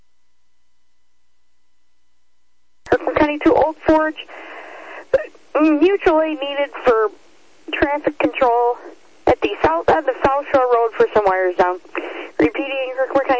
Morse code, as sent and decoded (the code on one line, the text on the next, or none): none